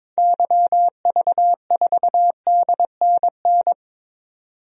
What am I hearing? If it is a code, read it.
Morse code, heard as a keyed tone, about 21 wpm, 700 Hz. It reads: YV4DNN
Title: DXCC-100 Call Signs - 6 Characters - 22wpm